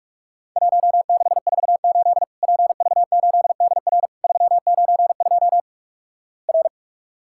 Morse code, heard as a keyed tone, about 44 wpm, 700 Hz. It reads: 1648 PV8DR 392 R
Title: Worked All Europe DX Contest - QTC Exchange - 45wpm